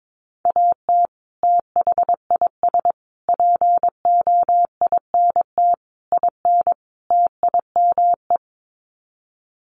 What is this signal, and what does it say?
Morse code, heard as a keyed tone, about 22 wpm, 700 Hz. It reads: AT THIS POINT IN TIME
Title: Sets of 5 Words Encore 22wpm